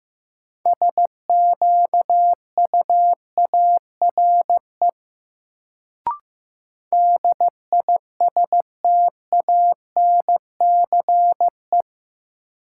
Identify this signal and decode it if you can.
Morse code, heard as a keyed tone, about 15 wpm, 700 Hz. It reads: SQUARE DISTANCE
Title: Sentences from Top 1200 Words - New Word Drill Alt - 15wpm